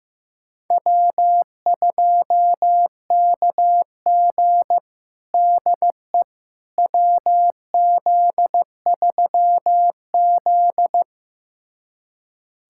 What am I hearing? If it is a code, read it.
Morse code, heard as a keyed tone, about 15 wpm, 700 Hz. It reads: W2KG DE WZ3Z
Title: USA Callsign DE Callsign - 4 Characters - 15wpm